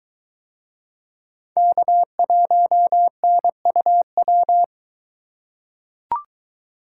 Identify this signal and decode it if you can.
Morse code, heard as a keyed tone, about 22 wpm, 700 Hz. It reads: K1NUW